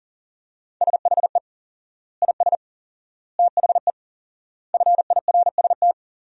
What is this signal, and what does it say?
Morse code, heard as a keyed tone, about 40 wpm, 700 Hz. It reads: SHE IS THE FIRST
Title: 4x Word-Spacing - Sentences from Top 400 Words - 40wpm